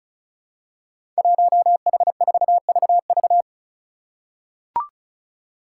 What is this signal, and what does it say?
Morse code, heard as a keyed tone, about 34 wpm, 700 Hz. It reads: 1H4VV
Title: Five Challenging Letter-Number Combinations 35wpm